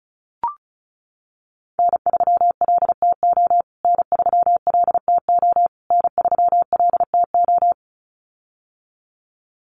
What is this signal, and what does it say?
Morse code, heard as a keyed tone, about 34 wpm, 700 Hz. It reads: N3LTO N3LTO N3LTO
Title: USA Call Signs Repeated 3x 35wpm